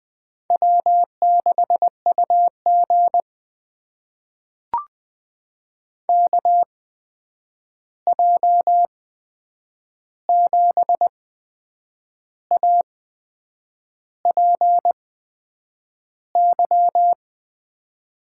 Morse code, heard as a keyed tone, about 21 wpm, spaced as for 8 wpm, 700 Hz. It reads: W6UG KJ7APY
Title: USA Call Signs - 8x Character Spacing - 20wpm